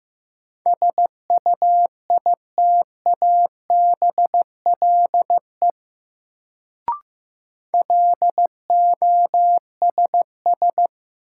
Morse code, heard as a keyed tone, about 15 wpm, 700 Hz. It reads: SUITABLE LOSS
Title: Sentences from Top 1400 Words - New Word Drill - 15wpm